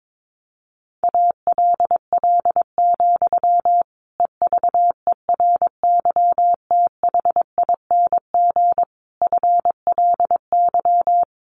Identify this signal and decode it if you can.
Morse code, heard as a keyed tone, about 22 wpm, 700 Hz. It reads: ALL, EVERYTHING FLY
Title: A Sound Of Thunder 22wpm